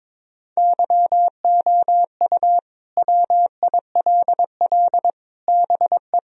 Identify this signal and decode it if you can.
Morse code, heard as a keyed tone, about 22 wpm, 700 Hz. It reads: YOU WILL BE
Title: Sentences from Top 1000 Words - Review Part 4 - 22wpm